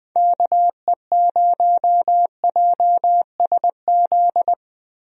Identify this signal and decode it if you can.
Morse code, heard as a keyed tone, about 20 wpm, 700 Hz. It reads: KE0JSZ
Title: USA Call Signs - 6 Characters - Repeated 2x - 20wpm